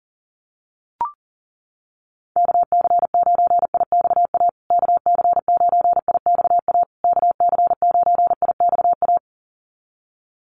Morse code, heard as a keyed tone, about 37 wpm, 700 Hz. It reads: KC9IXA KC9IXA KC9IXA
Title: USA Call Signs Repeated 3x 40wpm